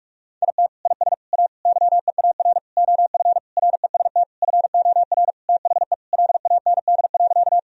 Morse code, heard as a keyed tone, about 46 wpm, 700 Hz. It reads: IT IS A YEAR OF REST FOR THE LAND.
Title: Sentences from Top 1000 Words - Review Part 1 - 45wpm